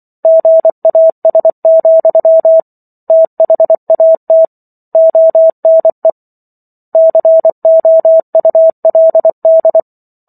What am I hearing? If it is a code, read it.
Morse code, heard as a keyed tone, about 24 wpm, 650 Hz. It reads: GAS, THAT ONE COULD